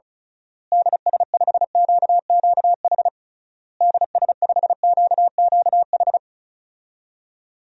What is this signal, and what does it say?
Morse code, heard as a keyed tone, about 36 wpm, 700 Hz. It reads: DS5QQH DS5QQH
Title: DXCC-100 Call Signs - 6 Characters - Repeated 3x - 35wpm